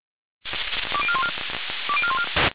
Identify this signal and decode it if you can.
Five-tone selective call (ZVEI), heard as quick runs of five tones, tones of about 70 ms. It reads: 20414 20414